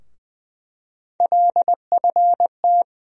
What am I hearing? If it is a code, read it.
Morse code, heard as a keyed tone, about 20 wpm, 700 Hz. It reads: LFT
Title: USA Call Sign Postfixes - 3 Letters 20wpm